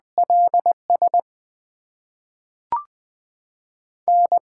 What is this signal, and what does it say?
Morse code, heard as a keyed tone, about 20 wpm, spaced as for 5 wpm, 700 Hz. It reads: LSN
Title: USA Call Signs - 5 Characters - 8x Character Spacing - 20wpm